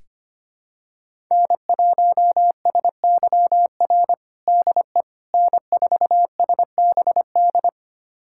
Morse code, heard as a keyed tone, about 25 wpm, 700 Hz. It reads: N1SYR DE N4SBD
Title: USA Callsign DE Callsign - 5 Characters - 25wpm